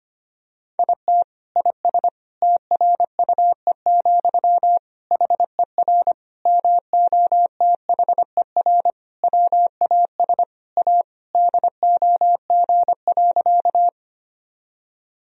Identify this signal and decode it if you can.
Morse code, heard as a keyed tone, about 25 wpm, 700 Hz. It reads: IT IS TRUE, HER MOTHER WAS A DOG.
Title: Sentences from Top 1000 Words - Review Part 2 - 25wpm